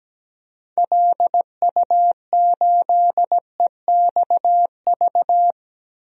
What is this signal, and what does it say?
Morse code, heard as a keyed tone, about 17 wpm, 700 Hz. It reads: LU8EXV
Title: DXCC-100 Call Signs - 6 Characters - 17wpm